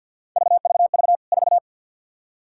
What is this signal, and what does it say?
Morse code, heard as a keyed tone, about 46 wpm, 700 Hz. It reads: VVV 4
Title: K1USN SST QSO Practice 50wpm